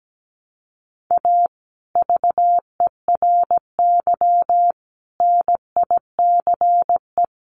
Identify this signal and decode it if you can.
Morse code, heard as a keyed tone, about 17 wpm, 700 Hz. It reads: A VERY NICE
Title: Sentences from Top 1200 Words 17wpm